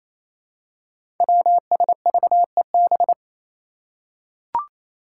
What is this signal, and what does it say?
Morse code, heard as a keyed tone, about 27 wpm, 700 Hz. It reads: WSVEB
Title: Five Challenging Letter-Number Combinations 28wpm